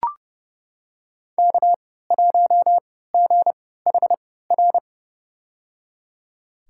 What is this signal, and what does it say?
Morse code, heard as a keyed tone, about 30 wpm, spaced as for 17 wpm, 700 Hz. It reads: K1GHR
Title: USA Call Signs - 5 Characters - 3x Character Spacing - 30wpm